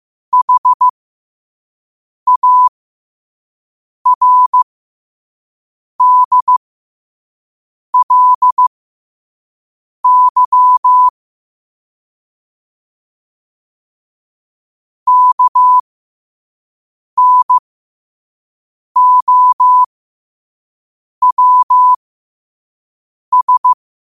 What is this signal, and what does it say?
Morse code, heard as a keyed tone, about 15 wpm, spaced as for 5 wpm, 1000 Hz. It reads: HARDLY KNOWS
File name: qotdfiles / 5wpm